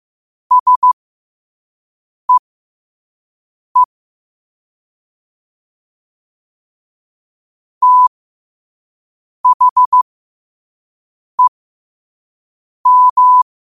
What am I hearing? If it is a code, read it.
Morse code, heard as a keyed tone, about 15 wpm, spaced as for 5 wpm, 1000 Hz. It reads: SEE THEM
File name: qotdfiles / 5wpm